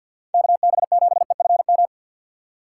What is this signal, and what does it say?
Morse code, heard as a keyed tone, about 47 wpm, 700 Hz. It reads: KB7EFM
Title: USA Call Signs - 6x Character Spacing - 50wpm